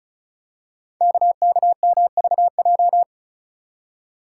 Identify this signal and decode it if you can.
Morse code, heard as a keyed tone, about 36 wpm, 700 Hz. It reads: KKMVJ
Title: Code Groups - 5 Letters - 35wpm